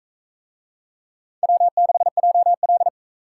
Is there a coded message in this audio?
Yes: Morse code from 1.4 to 2.9 s, heard as a keyed tone, about 40 wpm, 700 Hz. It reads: W6JL